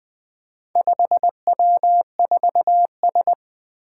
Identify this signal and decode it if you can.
Morse code, heard as a keyed tone, about 20 wpm, 700 Hz. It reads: 5W4S
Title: Four Challenging Letter-Number Combinations 20wpm